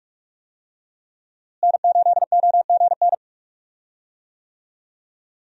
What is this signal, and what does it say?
Morse code, heard as a keyed tone, about 45 wpm, 700 Hz. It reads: N8OGN
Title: USA Call Signs - 5 Characters - Repeated 2x - 45wpm